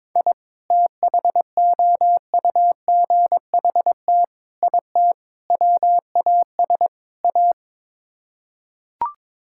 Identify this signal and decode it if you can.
Morse code, heard as a keyed tone, about 22 wpm, 700 Hz. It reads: I THOUGHT IT WAS A
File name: practice / 22wpm / Build A Phrase - 5-Word Phrases - 22wpm